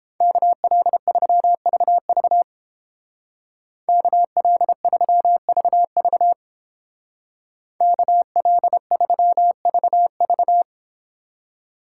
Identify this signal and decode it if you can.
Morse code, heard as a keyed tone, about 29 wpm, 700 Hz. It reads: KL3VV KL3VV KL3VV